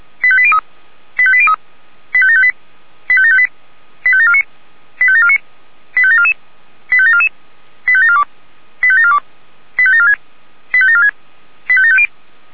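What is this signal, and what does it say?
Five-tone selective call (ZVEI), heard as quick runs of five tones, tones of about 70 ms. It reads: 85803 85803 85758 85758 85749 85749 85744 85744 85742 85742 85746 85746 85750
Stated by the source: Five-tone ZVEI selective call received by SDR.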